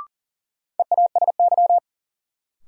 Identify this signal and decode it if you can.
Morse code, heard as a keyed tone, about 40 wpm, 700 Hz. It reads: EASY
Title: Sentences from Top 700 Words - New Word Drill - 40wpm